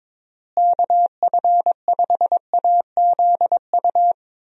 Morse code, heard as a keyed tone, about 22 wpm, 700 Hz. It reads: KF5AZU
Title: USA Call Signs - 6 Characters - Repeated 2x - 22wpm